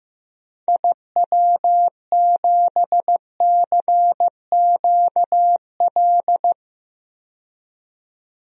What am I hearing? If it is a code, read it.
Morse code, heard as a keyed tone, about 16 wpm, 700 Hz. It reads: IW7CQL
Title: DXCC-100 Call Signs - 6 Characters - 15wpm